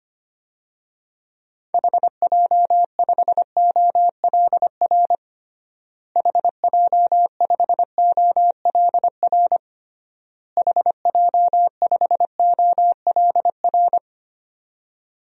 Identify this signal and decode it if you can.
Morse code, heard as a keyed tone, about 25 wpm, 700 Hz. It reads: HJ5OLR HJ5OLR HJ5OLR
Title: DXCC-100 Call Signs - 6 Characters - Repeated 3x - 25wpm